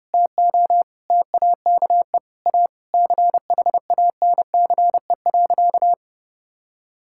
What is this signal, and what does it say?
Morse code, heard as a keyed tone, about 32 wpm, 700 Hz. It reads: TO TAKE A CHANCE.
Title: Sentences from Top 1000 Words - Review Part 6 - 30wpm